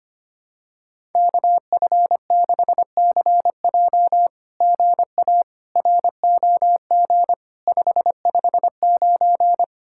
Morse code, heard as a keyed tone, about 25 wpm, 700 Hz. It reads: KF6CJ GA ROG 559
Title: SOTA QSO Practice 25wpm